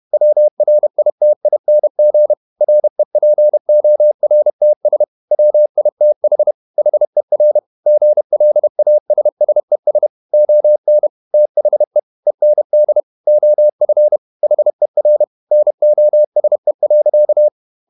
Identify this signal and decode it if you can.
Morse code, heard as a keyed tone, about 31 wpm, 600 Hz. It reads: WRITING REPORTS WITH HER GLASSES ON THE END OF HER NOSE.
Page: Never 31wpm | CW med Gnister